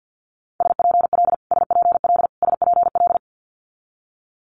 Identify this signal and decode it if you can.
Morse code, heard as a keyed tone, about 50 wpm, 700 Hz. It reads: SPL SPL SPL
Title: Multi-letter Phonemes Repeated 3x 50wpm